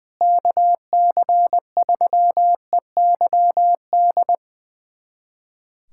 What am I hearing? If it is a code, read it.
Morse code, heard as a keyed tone, about 20 wpm, 700 Hz. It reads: KC3EYD
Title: USA Callsign DE Callsign 20wpm